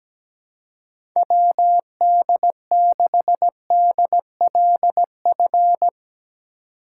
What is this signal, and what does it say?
Morse code, heard as a keyed tone, about 17 wpm, 700 Hz. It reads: WD6DLF
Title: USA Call Signs - 4x Character Spacing - 17wpm